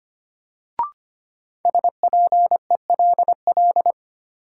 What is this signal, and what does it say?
Morse code, heard as a keyed tone, about 25 wpm, 700 Hz. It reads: SPELL
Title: Sentences from Top 1000 Words - Review Part 1 - New Word Drill - 25wpm